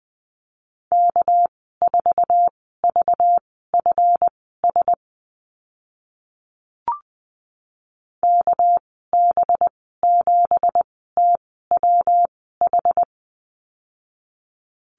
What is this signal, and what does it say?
Morse code, heard as a keyed tone, about 20 wpm, spaced as for 14 wpm, 700 Hz. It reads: K4VFS KB7TWH
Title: 245 Lesson 39g - Letter z - Call Signs 20wpm